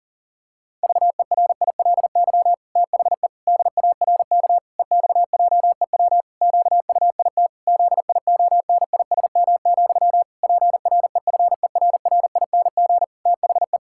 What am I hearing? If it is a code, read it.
Morse code, heard as a keyed tone, about 40 wpm, 700 Hz. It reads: VERILY THE DARK EXJEW QUIT ZIONISM, PREFERRING THE
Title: Pangram Challenge v2 - 40wpm